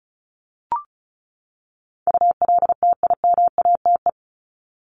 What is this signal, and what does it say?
Morse code, heard as a keyed tone, about 34 wpm, 700 Hz. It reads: ULTIMATE
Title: LY Suffix - Top 100 Words with Base Words 35wpm